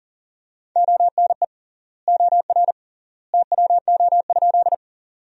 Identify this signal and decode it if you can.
Morse code, heard as a keyed tone, about 40 wpm, 700 Hz. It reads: ONE OR TWO?
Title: 3x Word-Spacing - Sentences from Top 100 Words - 40wpm